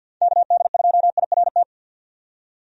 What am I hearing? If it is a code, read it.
Morse code, heard as a keyed tone, about 51 wpm, 700 Hz. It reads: KD2IRT
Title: USA Call Signs - 6 Characters - 50wpm